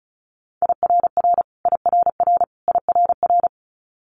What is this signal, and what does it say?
Morse code, heard as a keyed tone, about 36 wpm, 700 Hz. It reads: IRR IRR IRR
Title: Multi-letter Phonemes Repeated 3x 35wpm